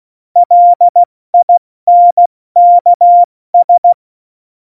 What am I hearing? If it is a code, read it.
Morse code, heard as a keyed tone, about 16 wpm, spaced as for 14 wpm, 700 Hz. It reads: LINKS
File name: Morsekurs / Witze / Witz 25. Dezember, 70 F80